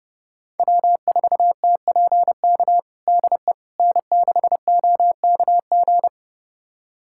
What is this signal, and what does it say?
Morse code, heard as a keyed tone, about 30 wpm, 700 Hz. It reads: W4TPK DE N6OKG
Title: USA Callsign DE Callsign - 5 Characters - 30wpm